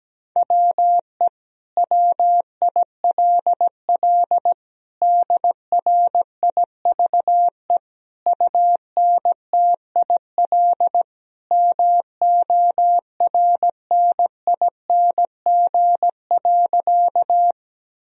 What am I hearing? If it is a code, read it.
Morse code, heard as a keyed tone, about 17 wpm, 700 Hz. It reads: WE WILL DRIVE UNTIL MORNING.
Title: Sentences from Top 1000 Words - Review Part 2 - 17wpm